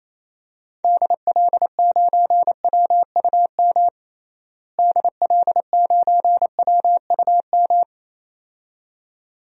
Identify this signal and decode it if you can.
Morse code, heard as a keyed tone, about 28 wpm, 700 Hz. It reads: DL9WUM DL9WUM
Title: DXCC-100 Call Signs - 6 Characters - Repeated 2x - 28wpm